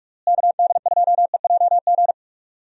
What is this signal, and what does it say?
Morse code, heard as a keyed tone, about 43 wpm, 700 Hz. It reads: KD2EJG
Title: USA Call Signs - 6 Characters - Repeated 3x - 45wpm